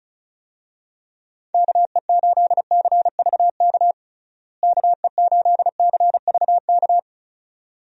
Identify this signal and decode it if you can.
Morse code, heard as a keyed tone, about 36 wpm, 700 Hz. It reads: KE8CVK KE8CVK
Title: USA Call Signs - 6 Characters - Repeated 2x - 35wpm